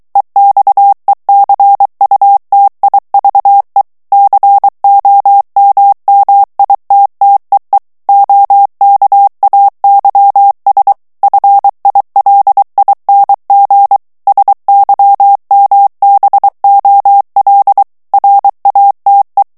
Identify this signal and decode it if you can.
Morse code, heard as a keyed tone, about 23 wpm, 800 Hz. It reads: EXECUTIVE COMMITTEE OKAYS FILING SYMBOL RATE
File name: ANGLAIS texte clair 3 vitesse 22 mots minute